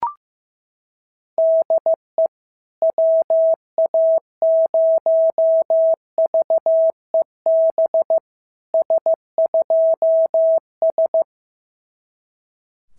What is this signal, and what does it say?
Morse code, heard as a keyed tone, about 15 wpm, 650 Hz. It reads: DE WA0VEB S2S
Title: SOTA QSO Practice 15wpm